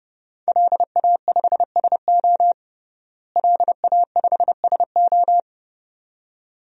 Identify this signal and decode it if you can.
Morse code, heard as a keyed tone, about 30 wpm, 700 Hz. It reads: LA5SO LA5SO
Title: DXCC-100 Call Signs - 5 Characters - Repeated 2x - 30wpm